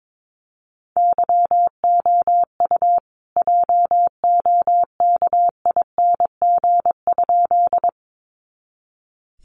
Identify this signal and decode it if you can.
Morse code, heard as a keyed tone, about 22 wpm, 700 Hz. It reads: YOU JOKING?
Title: A Sound Of Thunder 22wpm